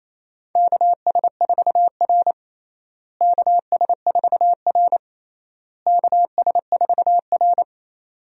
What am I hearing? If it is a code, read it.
Morse code, heard as a keyed tone, about 28 wpm, 700 Hz. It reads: KS4R KS4R KS4R
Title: USA Call Signs - 4 Characters - Repeated 3x - 28wpm